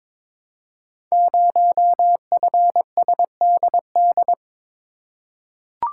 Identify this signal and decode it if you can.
Morse code, heard as a keyed tone, about 22 wpm, 700 Hz. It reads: 0FSDD
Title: Code Groups - 5 Letter:Numbers - 22wpm